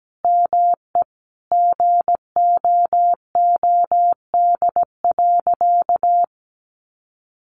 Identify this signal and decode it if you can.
Morse code, heard as a keyed tone, about 17 wpm, 700 Hz. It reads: ME GOOD.
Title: Sentences from Top 1300 Words 17wpm